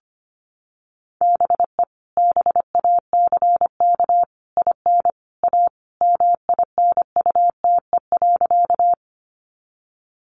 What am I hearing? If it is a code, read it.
Morse code, heard as a keyed tone, about 25 wpm, 700 Hz. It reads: BE BACK IN A MINUTE.
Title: Sentences from Top 500 Words 25wpm